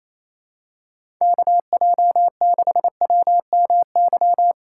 Morse code, heard as a keyed tone, about 28 wpm, 700 Hz. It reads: KJ6WMY
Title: USA Call Signs - 6 Characters - 3x Character Spacing - 28wpm